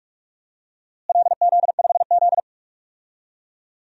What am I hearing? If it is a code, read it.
Morse code, heard as a keyed tone, about 46 wpm, 700 Hz. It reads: LZ5Z
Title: DXCC-100 Call Signs - 4 Characters - Repeated 3x - 45wpm